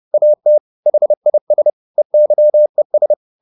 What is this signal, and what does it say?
Morse code, heard as a keyed tone, about 30 wpm, 600 Hz. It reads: AT HIS EYES
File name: the old man and the sea / the old man and the sea 1 30wpm